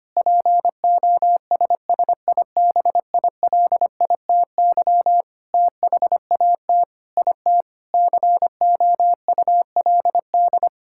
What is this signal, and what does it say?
Morse code, heard as a keyed tone, about 25 wpm, 700 Hz. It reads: POSSIBILITY THAT IT COULD
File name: practice / 25wpm / Sentences with 9-Word Phrases - 25wpm